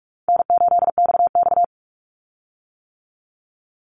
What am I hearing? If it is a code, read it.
Morse code, heard as a keyed tone, about 44 wpm, 700 Hz. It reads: N8XX
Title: Call Signs - 45wpm